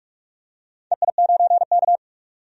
Morse code, heard as a keyed tone, about 46 wpm, 700 Hz. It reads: EI9K
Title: DXCC-100 Call Signs - 4 Characters - 45wpm